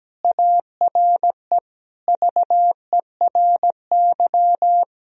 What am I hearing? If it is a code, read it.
Morse code, heard as a keyed tone, about 17 wpm, 700 Hz. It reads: ARE VERY
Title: Sentences from Top 1000 Words - Review Part 1 - 17wpm